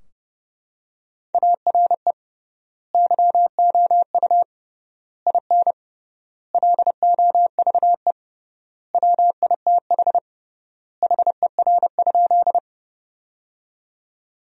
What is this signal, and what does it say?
Morse code, heard as a keyed tone, about 30 wpm, 700 Hz. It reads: ARE YOU IN LOVE WITH HER?
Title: 3x Word-Spacing - Sentences from Top 400 Words - 30wpm